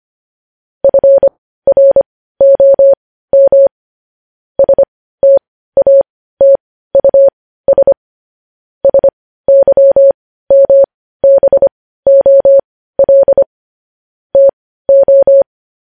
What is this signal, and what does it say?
Morse code, heard as a keyed tone, about 25 wpm, spaced as for 15 wpm, 550 Hz. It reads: FROM STATUS SYMBOL TO
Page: news headlines in morse code at 15 wpm